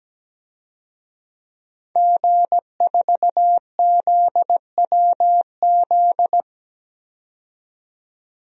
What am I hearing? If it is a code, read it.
Morse code, heard as a keyed tone, about 17 wpm, 700 Hz. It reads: G4ZWZ
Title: DXCC-100 Call Signs - 5 Characters - 17wpm